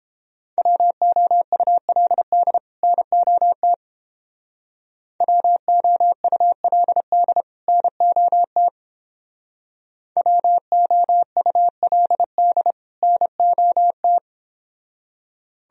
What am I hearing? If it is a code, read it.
Morse code, heard as a keyed tone, about 31 wpm, 700 Hz. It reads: WOULD NOT WOULD NOT WOULD NOT